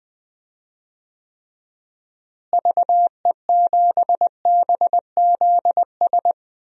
Morse code, heard as a keyed tone, about 20 wpm, 700 Hz. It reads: VE7BZS
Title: DXCC-100 Call Signs - 6 Characters - Repeated 2x - 20wpm